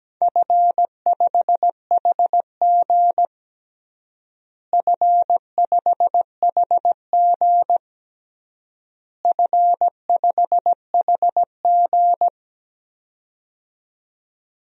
Morse code, heard as a keyed tone, about 17 wpm, 700 Hz. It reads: F5HG F5HG F5HG
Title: DXCC-100 Call Signs - 4 Characters - Repeated 3x - 17wpm